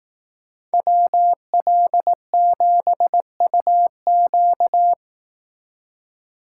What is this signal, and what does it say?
Morse code, heard as a keyed tone, about 18 wpm, 700 Hz. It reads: WL7UQ
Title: Speed Racing - USA 2x2 Call Signs 15wpm